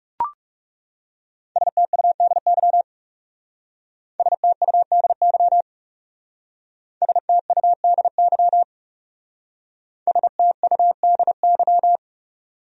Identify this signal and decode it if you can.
Morse code, heard as a keyed tone, about 36 wpm, 700 Hz. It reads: STUDY STUDY STUDY STUDY
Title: Speed-Racing Top 300 Words 30wpm